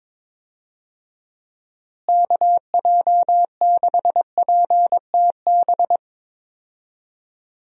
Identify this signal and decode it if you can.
Morse code, heard as a keyed tone, about 21 wpm, 700 Hz. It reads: KJ6PTB
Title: USA Call Signs - 6 Characters - Repeated 3x - 22wpm